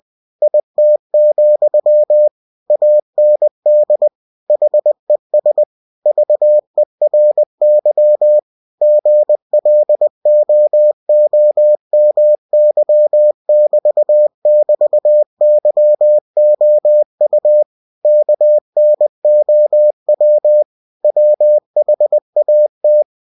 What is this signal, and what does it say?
Morse code, heard as a keyed tone, about 20 wpm, 600 Hz. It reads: IT, AND HES VERY GLOOMY<BT><BT>YOU KNOW WHAT
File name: cw-books / winnie-the-pooh / winnie-20wpm-page-0048